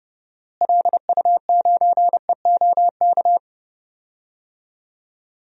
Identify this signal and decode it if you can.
Morse code, heard as a keyed tone, about 30 wpm, 700 Hz. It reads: LU9EOK
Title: DXCC-100 Call Signs - 6 Characters - Repeated 2x - 30wpm